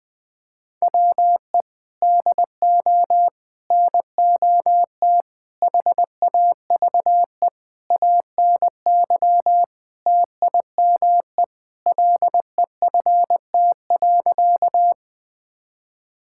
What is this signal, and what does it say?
Morse code, heard as a keyed tone, about 20 wpm, 700 Hz. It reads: WE DO NOT HAVE ANY TIME LEFT.
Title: Sentences with 4-Word Phrases - 20wpm